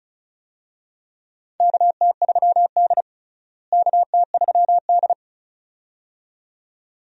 Morse code, heard as a keyed tone, about 36 wpm, 700 Hz. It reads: KT3D KT3D
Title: USA Call Signs - 4 Characters - Repeated 2x - 35wpm